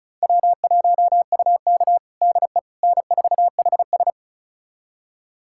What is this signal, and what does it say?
Morse code, heard as a keyed tone, about 34 wpm, 690 Hz. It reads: W1UK DE N4HS
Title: USA Callsign DE Callsign - 4 Characters - 35wpm